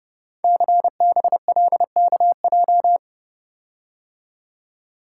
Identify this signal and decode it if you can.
Morse code, heard as a keyed tone, about 30 wpm, 700 Hz. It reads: CBLKJ
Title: Code Groups - 5 Letters - 30wpm